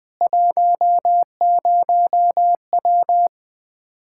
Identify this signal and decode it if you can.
Morse code, heard as a keyed tone, about 20 wpm, 700 Hz. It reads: 10W
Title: Practice QSO - Exchange 2 - Info - 20wpm